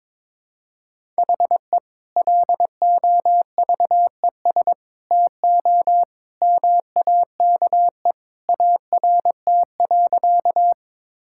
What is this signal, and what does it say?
Morse code, heard as a keyed tone, about 22 wpm, 700 Hz. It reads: HE LOVES TO MAKE ART.
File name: practice / 22wpm / Sentences with 2-Word Phrases - 22wpm